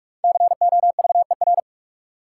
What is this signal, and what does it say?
Morse code, heard as a keyed tone, about 42 wpm, 700 Hz. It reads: COVER
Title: Sentences from Top 500 Words - New Word Drill - 45wpm